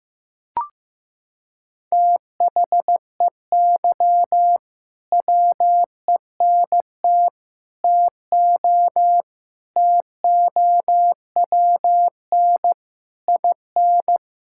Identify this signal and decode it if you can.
Morse code, heard as a keyed tone, about 16 wpm, 700 Hz. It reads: THEY WENT TO TOWN IN
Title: Sentences from Top 1000 Words - Review Part 5 - 15wpm